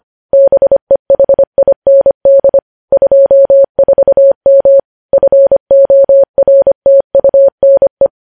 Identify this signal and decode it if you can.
Morse code, heard as a keyed tone, about 25 wpm, 550 Hz. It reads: BEHIND 24M FORTUNE
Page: news headlines in morse code at 25 wpm